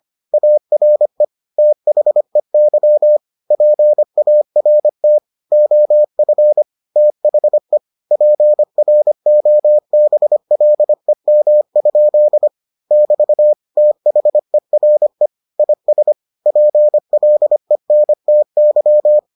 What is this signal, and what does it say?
Morse code, heard as a keyed tone, about 25 wpm, 600 Hz. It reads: ARE THEY PART OF THE PROBLEM? <BT> THERE IS PLENTY